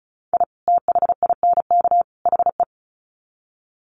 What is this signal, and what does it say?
Morse code, heard as a keyed tone, about 34 wpm, 700 Hz. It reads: I THINK HE
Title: Sets of 3 Words 35wpm